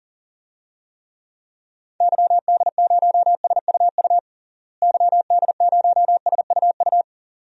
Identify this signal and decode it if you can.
Morse code, heard as a keyed tone, about 40 wpm, 700 Hz. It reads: YD0SUU YD0SUU
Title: DXCC-100 Call Signs - 6 Characters - Repeated 2x - 40wpm